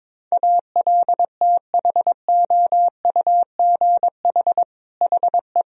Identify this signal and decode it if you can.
Morse code, heard as a keyed tone, about 21 wpm, 700 Hz. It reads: ALTHOUGH HE
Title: Sentences from Top 1000 Words - Review Part 3 - 22wpm